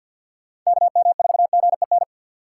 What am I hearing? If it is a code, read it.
Morse code, heard as a keyed tone, about 53 wpm, 700 Hz. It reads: KM4GEN
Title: USA Call Signs - 3x Character Spacing - 50wpm